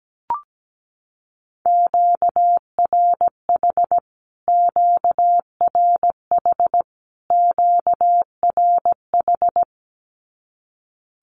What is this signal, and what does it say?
Morse code, heard as a keyed tone, about 17 wpm, 700 Hz. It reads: QRH QRH QRH
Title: Q-Codes Repeated 3x 17wpm